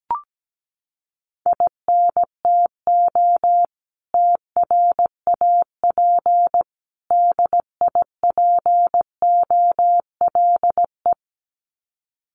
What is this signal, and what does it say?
Morse code, heard as a keyed tone, about 17 wpm, 700 Hz. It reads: INTO TRAP DIPOLE
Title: Common QSO Phrases 17wpm